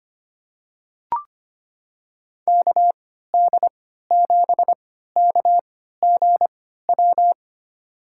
Morse code, heard as a keyed tone, about 25 wpm, spaced as for 14 wpm, 700 Hz. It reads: KD7KGW
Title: USA Call Signs - 6 Characters - 3x Character Spacing - 25wpm